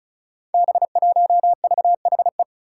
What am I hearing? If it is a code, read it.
Morse code, heard as a keyed tone, about 36 wpm, 700 Hz. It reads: B1VHE
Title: Five Challenging Letter-Number Combinations 35wpm